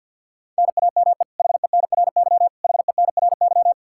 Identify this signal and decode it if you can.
Morse code, heard as a keyed tone, about 50 wpm, 700 Hz. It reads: NAME HENRY HENRY
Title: Practice QSO - Exchange 1 - Beginning - 50wpm